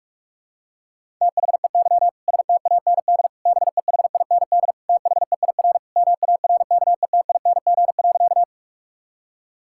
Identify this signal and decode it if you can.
Morse code, heard as a keyed tone, about 47 wpm, 700 Hz. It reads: THEY STAND BEHIND THEIR MARKETING.
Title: Sentences from Top 1000 Words - Review Part 5 - 45wpm